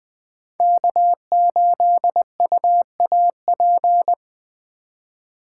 Morse code, heard as a keyed tone, about 20 wpm, 700 Hz. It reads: K8UAP
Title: Speed Racing - USA 1x3 Call Signs 17wpm